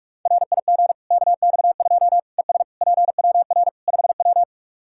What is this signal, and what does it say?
Morse code, heard as a keyed tone, about 46 wpm, 700 Hz. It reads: RIG KX2 ES PWR 5W
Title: K1USN SST QSO Practice 45wpm